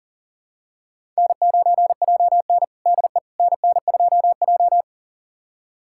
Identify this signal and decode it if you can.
Morse code, heard as a keyed tone, about 40 wpm, 700 Hz. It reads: N9JN DE NN2J
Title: USA Callsign DE Callsign - 4 Characters - 40wpm